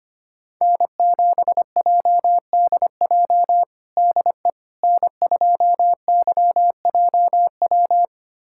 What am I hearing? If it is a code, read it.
Morse code, heard as a keyed tone, about 25 wpm, 700 Hz. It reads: N7JDJ DE N2YJW
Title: USA Callsign DE Callsign 25wpm